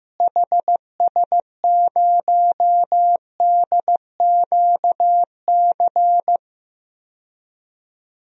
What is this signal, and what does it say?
Morse code, heard as a keyed tone, about 15 wpm, 700 Hz. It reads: HS0DQC
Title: DXCC-100 Call Signs - 6 Characters - Repeated 2x - 15wpm